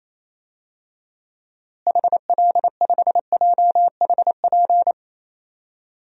Morse code, heard as a keyed tone, about 28 wpm, 700 Hz. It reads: HL5JHP
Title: DXCC-100 Call Signs - 6 Characters - Repeated 2x - 28wpm